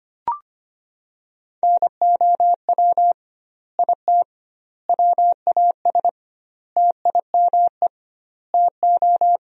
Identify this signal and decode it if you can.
Morse code, heard as a keyed tone, about 25 wpm, 700 Hz. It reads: NOW IT WAS TIME TO
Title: 2x Word-Spacing - Sentences from Top 200 Words - 25wpm